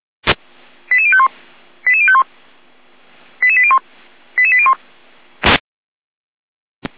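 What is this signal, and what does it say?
Five-tone selective call (ZVEI), heard as quick runs of five tones, tones of about 70 ms. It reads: 88051 88051 88801 88801
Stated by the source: ZVEI five-tone call received by SDR.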